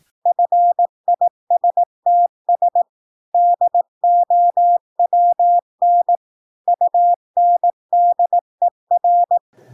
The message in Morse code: FISTS DOWN UNDER